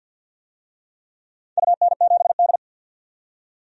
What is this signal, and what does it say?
Morse code, heard as a keyed tone, about 52 wpm, 700 Hz. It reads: UN7D
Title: DXCC-100 Call Signs - 4 Characters - 50wpm